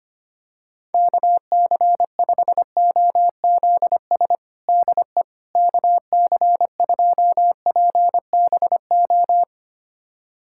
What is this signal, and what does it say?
Morse code, heard as a keyed tone, about 25 wpm, 700 Hz. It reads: KC5OZS DE KC2PBO
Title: USA Callsign DE Callsign 25wpm